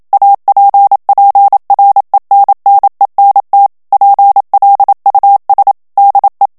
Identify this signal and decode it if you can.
Morse code, heard as a keyed tone, about 27 wpm, 800 Hz. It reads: APPRENNENT PLUS DE
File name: CW / FICHIERS-MP3 / FRANCAIS TEXTE CLAIR 1 VITESSE 26 MOTS MINUTES